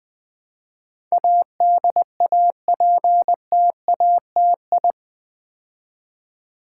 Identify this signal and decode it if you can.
Morse code, heard as a keyed tone, about 20 wpm, 700 Hz. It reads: ADAPTATI
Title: Build A Word - 4 Letters At A Time - 10-Letter Words - 20wpm